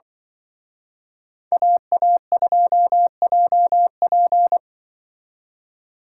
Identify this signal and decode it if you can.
Morse code, heard as a keyed tone, about 24 wpm, 700 Hz. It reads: AA2JP
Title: Speed Racing - USA 2x2 Call Signs 20wpm